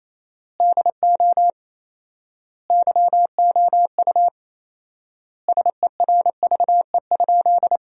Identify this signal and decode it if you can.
Morse code, heard as a keyed tone, about 28 wpm, 700 Hz. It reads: DO YOU SERVE?
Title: 4x Word-Spacing - Sentences from Top 400 Words - 28wpm